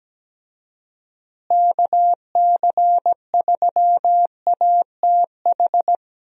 Morse code, heard as a keyed tone, about 17 wpm, 700 Hz. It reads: KC3ATH
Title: USA Call Signs - 6 Characters - Repeated 2x - 17wpm